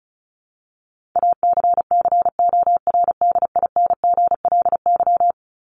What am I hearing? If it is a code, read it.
Morse code, heard as a keyed tone, about 34 wpm, 700 Hz. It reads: ACCORDINGLY
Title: LY Suffix - Top 100 Words with Base Words 35wpm